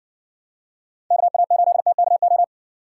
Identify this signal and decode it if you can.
Morse code, heard as a keyed tone, about 60 wpm, 700 Hz. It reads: KA8TKO